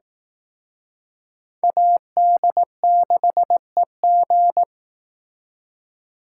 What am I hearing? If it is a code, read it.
Morse code, heard as a keyed tone, about 18 wpm, 700 Hz. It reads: AD6EG
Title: Speed Racing - USA 2x2 Call Signs 15wpm